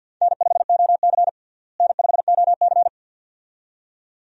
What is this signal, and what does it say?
Morse code, heard as a keyed tone, about 46 wpm, 700 Hz. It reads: N5OC N5OC
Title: USA Call Signs - 4 Characters - Repeated 2x - 50wpm